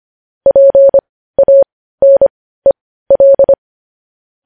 Morse code, heard as a keyed tone, about 25 wpm, spaced as for 15 wpm, 550 Hz. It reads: PANEL
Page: News Headlines in Morse Code at 15 WPM